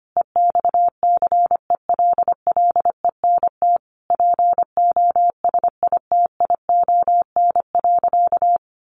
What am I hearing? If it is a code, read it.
Morse code, heard as a keyed tone, about 25 wpm, 700 Hz. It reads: EXCELLENT POSITION.
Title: Sentences from Top 1000 Words 25wpm